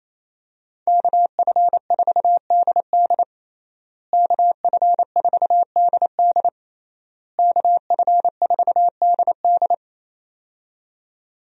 Morse code, heard as a keyed tone, about 28 wpm, 700 Hz. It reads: KF4DD KF4DD KF4DD
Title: USA Call Signs - 5 Characters - Repeated 3x - 28wpm